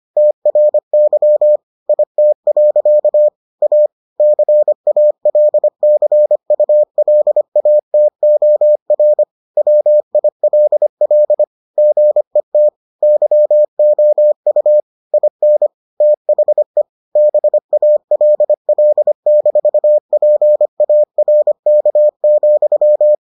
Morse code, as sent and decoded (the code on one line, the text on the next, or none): - .-. -.-- / .. - .-.-.- / .- / -.-. .- .-.. -.-. ..- .-.. .- - --- .-. / .-- .. .-.. .-.. / --. - / -.-- --- ..- / .. -. / - .... / -... .- .-.. .-.. -....- .--. .- .-. -.- --..--
TRY IT. A CALCULATOR WILL GET YOU IN THE BALL-PARK,